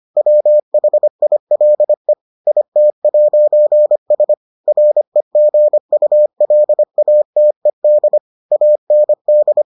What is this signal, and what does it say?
Morse code, heard as a keyed tone, about 25 wpm, 600 Hz. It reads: WHILE IT'S REGULATED AND